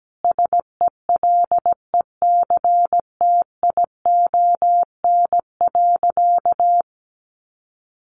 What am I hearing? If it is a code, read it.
Morse code, heard as a keyed tone, about 17 wpm, 700 Hz. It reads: SELECTION.
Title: Sentences from Top 1400 Words 17wpm